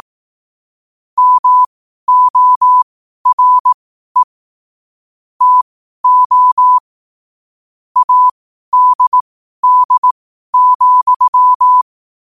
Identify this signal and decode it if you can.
Morse code, heard as a keyed tone, about 18 wpm, spaced as for 12 wpm, 1000 Hz. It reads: MORE TO ADD,
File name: qotdfiles / 12wpm